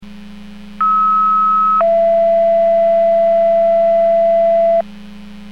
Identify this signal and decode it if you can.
Two-tone page, heard as a one-second tone, then a three-second tone, about 1290 then 670 Hz. It is heard from 0.8 to 4.8 s.